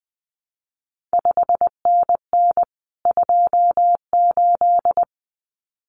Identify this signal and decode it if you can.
Morse code, heard as a keyed tone, about 20 wpm, 700 Hz. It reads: 5NN 28
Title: CQ WW DX Contest Exchange Encore - 20wpm